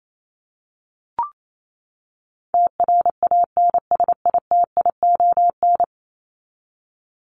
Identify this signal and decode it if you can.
Morse code, heard as a keyed tone, about 28 wpm, 700 Hz. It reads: TRANSITION
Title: Top 1751-2000 Words 28wpm